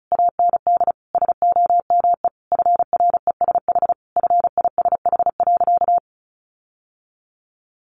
Morse code, heard as a keyed tone, about 34 wpm, 700 Hz. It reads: AND SOME FRESH FISH.
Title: Sentences from Top 1500 Words 35wpm